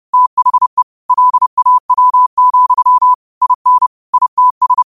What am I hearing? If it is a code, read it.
Morse code, heard as a keyed tone, about 30 wpm, 1000 Hz. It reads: THE LAW, IN ITS
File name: qotdfiles / 30wpm